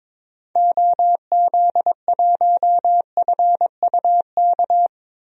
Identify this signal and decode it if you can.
Morse code, heard as a keyed tone, about 21 wpm, 700 Hz. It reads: OZ1FUK
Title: DXCC-100 Call Signs - 6 Characters - 22wpm